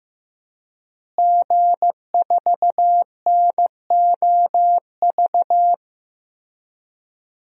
Morse code, heard as a keyed tone, about 15 wpm, 700 Hz. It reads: G4NOV
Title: DXCC-100 Call Signs - 5 Characters - Repeated 2x - 15wpm